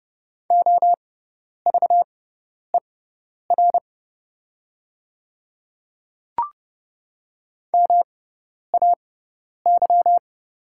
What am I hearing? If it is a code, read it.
Morse code, heard as a keyed tone, about 30 wpm, spaced as for 10 wpm, 690 Hz. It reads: OVER MAY
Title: 6x Character Spacing - Top 300 Words - 30wpm